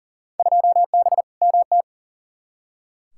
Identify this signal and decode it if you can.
Morse code, heard as a keyed tone, about 38 wpm, 700 Hz. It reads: 2B MT
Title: Field Day Exchanges Encore - 40wpm